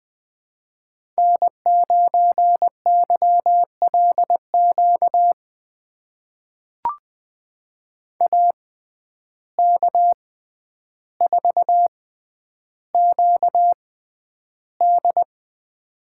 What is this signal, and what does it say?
Morse code, heard as a keyed tone, about 20 wpm, spaced as for 10 wpm, 700 Hz. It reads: N9YLQ AK4QD
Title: USA Call Signs - 5 Characters - 6x Character Spacing - 20wpm